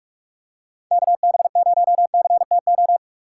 Morse code, heard as a keyed tone, about 46 wpm, 700 Hz. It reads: KB0CTO
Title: USA Call Signs - 6 Characters - 6x Character Spacing - 45wpm